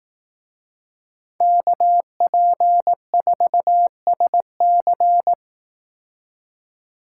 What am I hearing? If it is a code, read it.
Morse code, heard as a keyed tone, about 18 wpm, 700 Hz. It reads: KP4SC
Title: Speed Racing - USA 2x2 Call Signs 15wpm